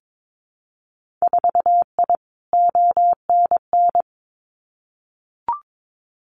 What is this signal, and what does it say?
Morse code, heard as a keyed tone, about 22 wpm, 700 Hz. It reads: 4I ONN
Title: Winter Field Day Exchanges Encore 22wpm